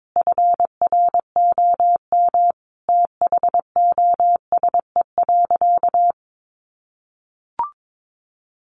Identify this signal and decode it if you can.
Morse code, heard as a keyed tone, about 22 wpm, 700 Hz. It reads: FROM THOSE.
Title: A Sound Of Thunder 22wpm